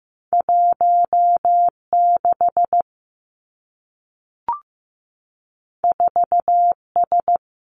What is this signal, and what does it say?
Morse code, heard as a keyed tone, about 15 wpm, 700 Hz. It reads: 1 6 4 S
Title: Two Challenging Letter-Number Combinations 15wpm